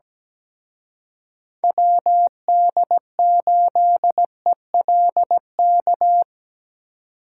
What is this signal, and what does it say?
Morse code, heard as a keyed tone, about 19 wpm, 700 Hz. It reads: WD8ELK
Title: Speed Racing - USA 2x3 Call Signs 17wpm